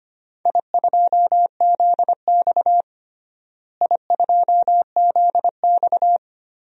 Morse code, heard as a keyed tone, about 25 wpm, 700 Hz. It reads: I2ZX I2ZX
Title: DXCC-100 Call Signs - 4 Characters - Repeated 2x - 25wpm